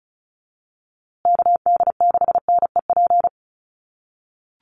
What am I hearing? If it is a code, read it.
Morse code, heard as a keyed tone, about 34 wpm, 700 Hz. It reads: KD6NEP
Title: USA Call Signs Repeated 3x 35wpm